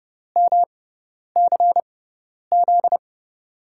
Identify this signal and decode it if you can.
Morse code, heard as a keyed tone, about 30 wpm, spaced as for 10 wpm, 700 Hz. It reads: MCZ
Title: USA Call Signs - 6 Characters - 6x Character Spacing - 30wpm